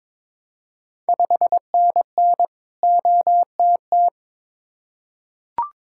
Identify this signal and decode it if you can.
Morse code, heard as a keyed tone, about 22 wpm, 700 Hz. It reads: 5NN OTT
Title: Ontario QSO Party Exchange 22wpm